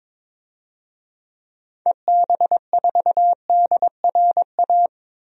Morse code, heard as a keyed tone, about 22 wpm, 700 Hz. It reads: EB4DRA
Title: DXCC-100 Call Signs - 6 Characters - Repeated 3x - 22wpm